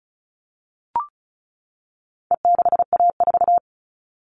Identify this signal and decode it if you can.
Morse code, heard as a keyed tone, about 34 wpm, 700 Hz. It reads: E6A4
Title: All Cut Numbers - 4-Digit 35wpm